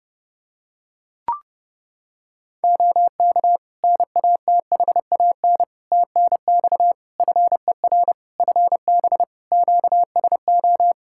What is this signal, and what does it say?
Morse code, heard as a keyed tone, about 30 wpm, 700 Hz. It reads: OK NATHAN TNX FER FB QSO
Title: Practice QSO - Exchange 4 - Ending - 30wpm